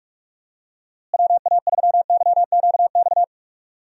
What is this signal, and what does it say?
Morse code, heard as a keyed tone, about 42 wpm, 700 Hz. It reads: WA3YQX
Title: USA Call Signs - 6 Characters - Repeated 2x - 45wpm